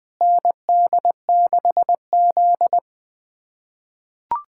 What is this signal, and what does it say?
Morse code, heard as a keyed tone, about 21 wpm, 700 Hz. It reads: ND6Z
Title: USA Callsign DE Callsign - 4 Characters - 20wpm